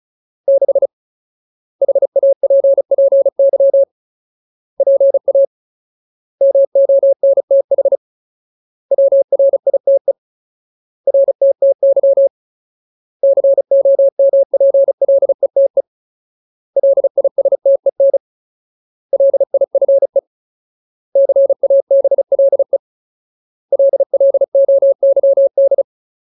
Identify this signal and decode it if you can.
Morse code, heard as a keyed tone, about 34 wpm, 550 Hz. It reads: 6 HAPPY PA MONTH WRITE RTTY COMPLETE LISTEN LIFE CABLE LLOYD
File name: Ultimate Word List / 35WPM Ultimate